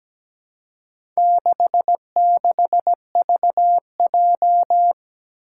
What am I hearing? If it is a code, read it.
Morse code, heard as a keyed tone, about 17 wpm, 700 Hz. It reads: 66VJ
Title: Four Challenging Letter-Number Combinations 17wpm